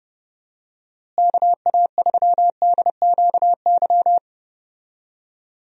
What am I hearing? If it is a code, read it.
Morse code, heard as a keyed tone, about 30 wpm, 700 Hz. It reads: KA3DQY
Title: USA Call Signs - 6 Characters - Repeated 3x - 30wpm